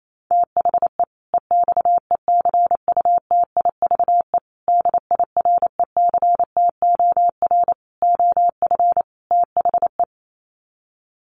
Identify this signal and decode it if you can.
Morse code, heard as a keyed tone, about 28 wpm, 700 Hz. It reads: THE EXECUTIVE DIRECTOR OF THE
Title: Sets of 5 Words Encore 28wpm